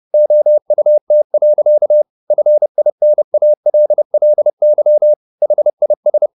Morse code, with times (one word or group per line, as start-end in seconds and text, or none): 0.1-2.0 OUT.
2.3-5.1 FINALLY
5.4-6.3 HIS